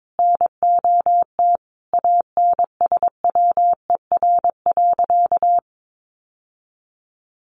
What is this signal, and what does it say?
Morse code, heard as a keyed tone, about 22 wpm, 700 Hz. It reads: NOT ANSWER.
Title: Sentences from Top 800 Words 22wpm